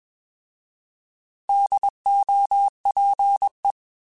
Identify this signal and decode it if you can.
Morse code, heard as a keyed tone, about 21 wpm, 800 Hz. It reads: DOPE
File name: Alphabetical / Alphabetical mp3s / Ds @20